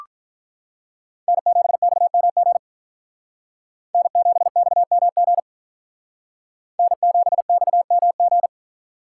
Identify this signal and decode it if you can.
Morse code, heard as a keyed tone, about 42 wpm, 700 Hz. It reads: N7XMG N7XMG N7XMG
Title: Speed Racing - USA 1x3 Call Signs 35wpm